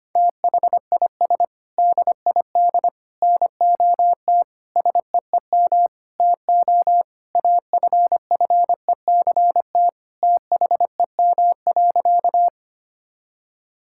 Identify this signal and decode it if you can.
Morse code, heard as a keyed tone, about 25 wpm, 700 Hz. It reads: THIS DID NOT SEEM TO AFFECT THEM.
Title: Sentences from Top 1000 Words - Review Part 6 - 25wpm